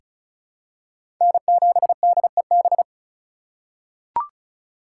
Morse code, heard as a keyed tone, about 34 wpm, 700 Hz. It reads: N7DEB
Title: USA Call Signs - 5 Characters - 2x Character Spacing - 35wpm